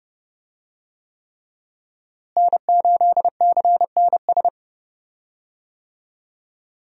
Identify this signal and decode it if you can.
Morse code, heard as a keyed tone, about 30 wpm, 700 Hz. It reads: N8CNS
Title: USA Call Signs - 5 Characters - Repeated 3x - 30wpm